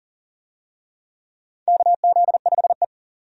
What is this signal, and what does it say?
Morse code, heard as a keyed tone, about 40 wpm, 700 Hz. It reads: KZ5E
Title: USA Call Signs - 4 Characters - 40wpm